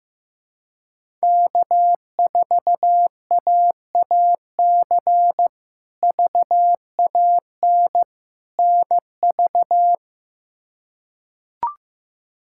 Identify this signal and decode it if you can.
Morse code, heard as a keyed tone, about 15 wpm, 700 Hz. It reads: K4AAC VAN NV
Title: K1USN SST QSO Practice - 15wpm